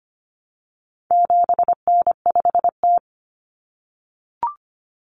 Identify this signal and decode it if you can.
Morse code, heard as a keyed tone, about 25 wpm, 700 Hz. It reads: 7N5T
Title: Common Cut Numbers - 4-Digit 25wpm